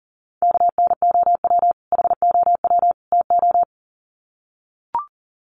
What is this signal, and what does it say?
Morse code, heard as a keyed tone, about 40 wpm, 700 Hz. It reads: KNOW HOW TO
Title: Sets of 3 Words 40wpm